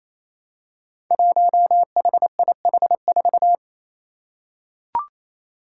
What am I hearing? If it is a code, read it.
Morse code, heard as a keyed tone, about 28 wpm, 700 Hz. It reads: 1HIH4
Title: Five Challenging Letter-Number Combinations 28wpm